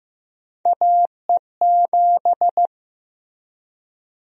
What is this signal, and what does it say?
Morse code, heard as a keyed tone, about 15 wpm, 700 Hz. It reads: AE7
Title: USA Call Sign Prefixes - 15wpm